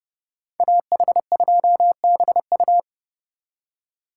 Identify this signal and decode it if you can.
Morse code, heard as a keyed tone, about 36 wpm, 700 Hz. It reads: AH2BU
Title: Speed Racing - USA 2x2 Call Signs 30wpm